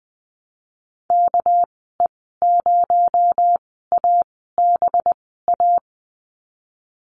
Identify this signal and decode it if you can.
Morse code, heard as a keyed tone, about 20 wpm, spaced as for 15 wpm, 700 Hz. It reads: KE0ABA
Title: 219 Lesson 35g - Number 0 - Call Signs 20wpm